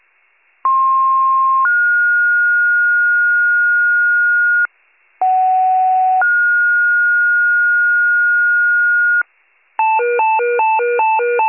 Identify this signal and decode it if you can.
Two-tone page, heard as a one-second tone, then a three-second tone, about 1060 then 1430 Hz, about 750 then 1430 Hz. It is heard from 0.6 to 9.2 s.